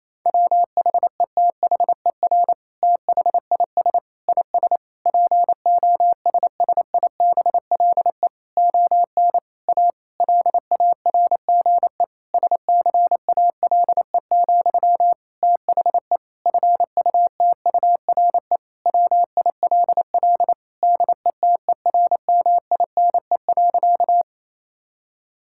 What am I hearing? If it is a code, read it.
Morse code, heard as a keyed tone, about 28 wpm, 700 Hz. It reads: WHETHER THIS IS POSSIBLE ON A LARGE SCALE, THE FUTURE WILL DETERMINE.
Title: Sentences from Top 1000 Words - Review Part 5 - 28wpm